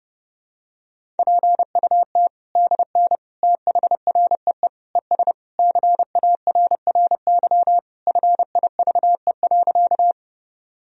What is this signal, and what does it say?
Morse code, heard as a keyed tone, about 32 wpm, 700 Hz. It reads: PUT DN THREE ES CARRY FIVE.
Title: Sentences from Top 400 Words and Abbreviations 30wpm